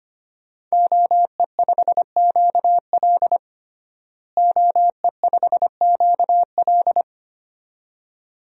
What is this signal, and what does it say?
Morse code, heard as a keyed tone, about 25 wpm, 700 Hz. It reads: OE5QL OE5QL
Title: DXCC-100 Call Signs - 5 Characters - Repeated 2x - 25wpm